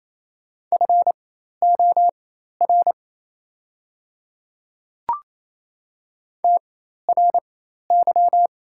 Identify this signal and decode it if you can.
Morse code, heard as a keyed tone, about 27 wpm, spaced as for 13 wpm, 700 Hz. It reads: FOR TRY
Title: 4x Character Spacing - Top 300 Words - 28wpm